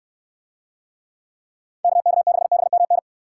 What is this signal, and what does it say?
Morse code, heard as a keyed tone, about 69 wpm, 700 Hz. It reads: KK6BMN